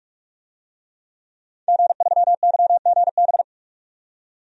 Morse code, heard as a keyed tone, about 46 wpm, 700 Hz. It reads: G3YGB
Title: DXCC-100 Call Signs - 5 Characters - 45wpm